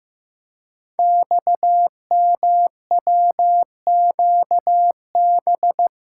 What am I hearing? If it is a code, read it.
Morse code, heard as a keyed tone, about 15 wpm, 700 Hz. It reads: XMWQB
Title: Code Groups - 5 Letter:Numbers - 15wpm